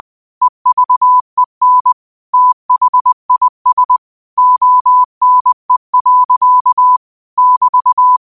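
Morse code, heard as a keyed tone, about 20 wpm, 1000 Hz. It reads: EVEN THIS ONE. <BT>
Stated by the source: Quote of the day sent in Morse at 20 words per minute.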